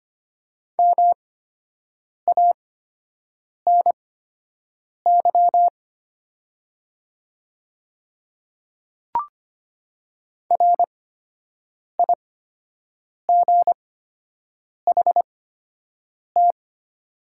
Morse code, heard as a keyed tone, about 25 wpm, spaced as for 7 wpm, 700 Hz. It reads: MANY RIGHT
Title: 8x Character Spacing - Top 200 Words - 25wpm